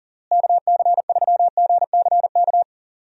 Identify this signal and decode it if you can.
Morse code, heard as a keyed tone, about 40 wpm, 700 Hz. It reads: KC3GCK
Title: USA Call Signs - 6 Characters - Repeated 3x - 40wpm